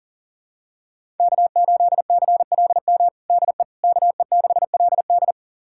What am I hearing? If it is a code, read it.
Morse code, heard as a keyed tone, about 40 wpm, 700 Hz. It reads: K8CLM DE KE6LD
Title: USA Callsign DE Callsign - 5 Characters - 40wpm